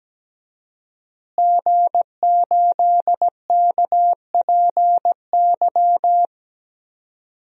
Morse code, heard as a keyed tone, about 17 wpm, 700 Hz. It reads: G8KPY
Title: DXCC-100 Call Signs - 5 Characters - Repeated 2x - 17wpm